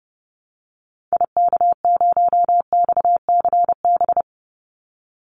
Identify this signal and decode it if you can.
Morse code, heard as a keyed tone, about 30 wpm, 700 Hz. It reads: IK0XCB
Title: Call Signs - 30wpm